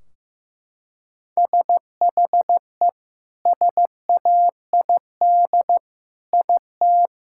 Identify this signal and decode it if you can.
Morse code, heard as a keyed tone, about 15 wpm, 700 Hz. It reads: SHE SAID IT
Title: Sentences from Top 1600 Words 15wpm